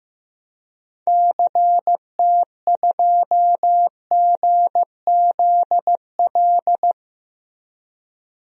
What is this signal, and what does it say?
Morse code, heard as a keyed tone, about 16 wpm, 700 Hz. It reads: CT2GZL
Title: DXCC-100 Call Signs - 6 Characters - Repeated 2x - 15wpm